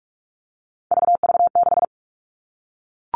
Morse code, heard as a keyed tone, about 44 wpm, 700 Hz. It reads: VV6
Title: All Cut Numbers - 3-Digit 45wpm